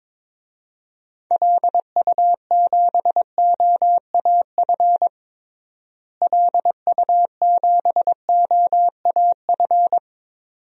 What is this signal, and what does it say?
Morse code, heard as a keyed tone, about 22 wpm, 700 Hz. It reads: LU7OAF LU7OAF
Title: DXCC-100 Call Signs - 6 Characters - Repeated 2x - 22wpm